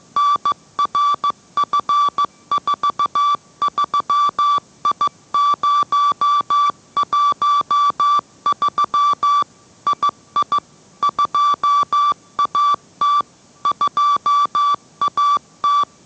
Morse code, heard as a keyed tone, about 15 wpm, 1160 Hz. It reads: NRF43I013 II 2AT 2AT